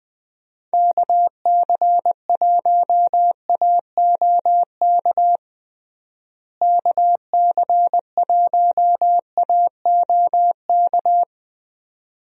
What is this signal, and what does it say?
Morse code, heard as a keyed tone, about 21 wpm, 700 Hz. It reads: KC1AOK KC1AOK
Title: USA Call Signs - 6 Characters - Repeated 2x - 20wpm